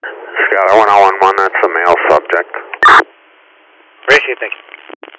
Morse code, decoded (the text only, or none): none